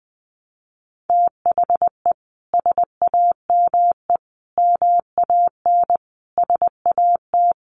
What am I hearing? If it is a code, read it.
Morse code, heard as a keyed tone, about 20 wpm, 700 Hz. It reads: THE SAME MAN SAT
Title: A Sound Of Thunder 20wpm